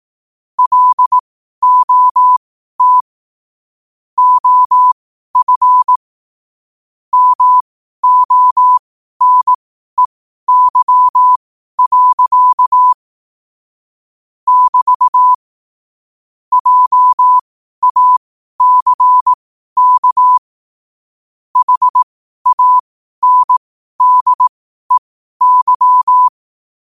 Morse code, heard as a keyed tone, about 18 wpm, spaced as for 12 wpm, 1000 Hz. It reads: LOT OF MONEY. <BT> JACK HANDEY
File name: qotdfiles / 12wpm